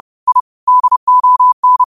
Morse code, heard as a keyed tone, about 30 wpm, 1000 Hz. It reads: I DON
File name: qotdfiles / 30wpm